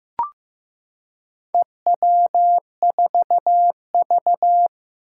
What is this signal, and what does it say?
Morse code, heard as a keyed tone, about 15 wpm, 700 Hz. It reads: EW4V
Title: Four Challenging Letter-Number Combinations 15wpm